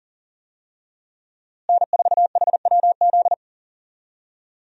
Morse code, heard as a keyed tone, about 40 wpm, 700 Hz. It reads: N4HWZ
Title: USA Call Signs - 5 Characters - Repeated 2x - 40wpm